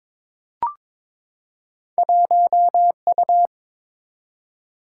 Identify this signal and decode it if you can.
Morse code, heard as a keyed tone, about 22 wpm, 700 Hz. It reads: 1U
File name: practice / 22wpm / Two Letter-Number Combinations - 22wpm